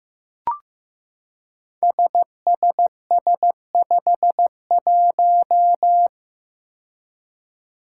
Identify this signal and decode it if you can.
Morse code, heard as a keyed tone, about 15 wpm, 700 Hz. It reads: SSS51
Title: Five Challenging Letter-Number Combinations 15wpm